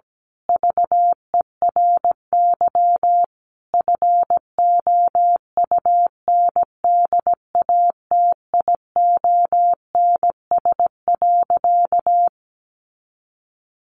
Morse code, heard as a keyed tone, about 17 wpm, 700 Hz. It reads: VERY FOUNDATIONS.
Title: A Sound Of Thunder 17wpm